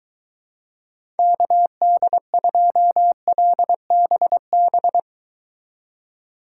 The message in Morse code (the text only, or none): KD2LBB